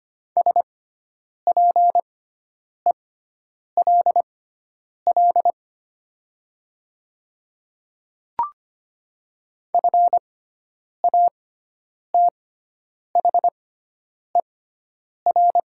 Morse code, heard as a keyed tone, about 25 wpm, spaced as for 9 wpm, 700 Hz. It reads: SPELL FATHER
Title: 6x Character Spacing - Top 200 Words - 25wpm